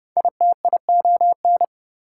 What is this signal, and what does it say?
Morse code, heard as a keyed tone, about 28 wpm, 700 Hz. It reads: ITION